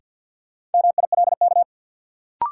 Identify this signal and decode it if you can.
Morse code, heard as a keyed tone, about 50 wpm, 700 Hz. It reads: MILK
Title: Sentences from Top 1500 Words - New Word Drill - 50wpm